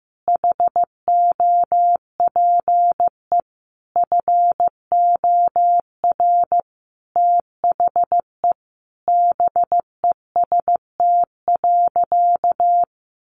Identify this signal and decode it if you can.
Morse code, heard as a keyed tone, about 14 wpm, 700 Hz. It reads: HOPE FOR THE BEST.
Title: Sentences from Top 1000 Words 15wpm